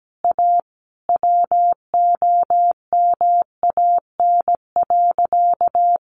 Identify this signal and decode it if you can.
Morse code, heard as a keyed tone, about 17 wpm, 700 Hz. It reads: A WOMAN.
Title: Sentences from Top 1100 Words 17wpm